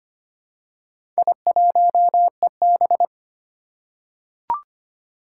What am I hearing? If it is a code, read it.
Morse code, heard as a keyed tone, about 25 wpm, 700 Hz. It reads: I1EB
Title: Four Challenging Letter-Number Combinations 25wpm